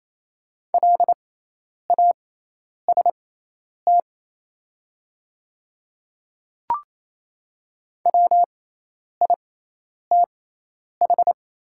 Morse code, heard as a keyed tone, about 28 wpm, spaced as for 10 wpm, 700 Hz. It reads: LAST WITH
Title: 6x Character Spacing - Top 300 Words - 28wpm